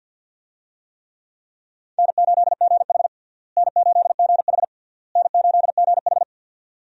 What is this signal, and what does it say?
Morse code, heard as a keyed tone, about 52 wpm, 700 Hz. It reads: N8GH N8GH N8GH
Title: USA Call Signs - 4 Characters - Repeated 3x - 50wpm